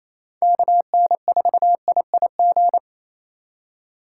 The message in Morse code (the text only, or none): KN4IIG